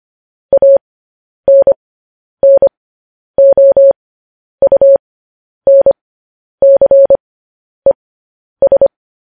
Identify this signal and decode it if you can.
Morse code, heard as a keyed tone, about 25 wpm, spaced as for 10 wpm, 550 Hz. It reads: ANNOUNCES
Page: news-headlines-in-morse-code-at-10-wpm